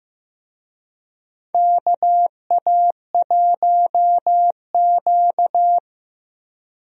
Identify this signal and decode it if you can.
Morse code, heard as a keyed tone, about 16 wpm, 700 Hz. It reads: KA1Q
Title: USA Call Signs - 4 Characters - Repeated 2x - 15wpm